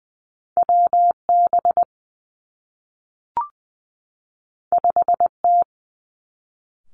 Morse code, heard as a keyed tone, about 20 wpm, 700 Hz. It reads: W B 5 T
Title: Two Challenging Letter-Number Combinations 20wpm